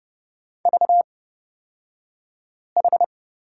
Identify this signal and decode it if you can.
Morse code, heard as a keyed tone, about 30 wpm, spaced as for 5 wpm, 700 Hz. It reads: VH
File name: Lesson 38d - Letter q - Two Characters 30wpm